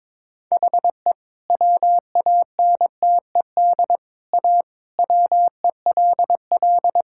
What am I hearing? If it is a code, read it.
Morse code, heard as a keyed tone, about 22 wpm, 700 Hz. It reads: HE WANTED A WELL
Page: morse-code-ninja-practice-22wpm